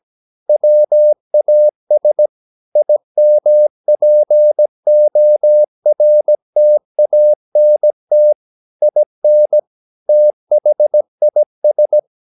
Morse code, heard as a keyed tone, about 17 wpm, 600 Hz. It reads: WAS IMPORTANT IN THIS
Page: Romanen Never af Ken Follet på engelsk.